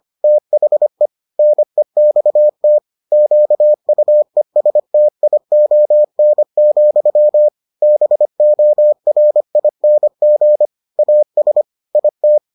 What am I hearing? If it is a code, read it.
Morse code, heard as a keyed tone, about 25 wpm, 600 Hz. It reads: THE NEXT QUESTION, BORING AS IT